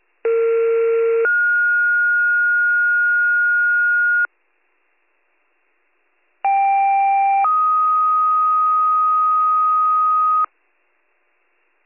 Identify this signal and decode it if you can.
Two-tone page, heard as a one-second tone, then a three-second tone, about 460 then 1430 Hz, about 790 then 1250 Hz. It is heard from 0.2 to 10.4 s.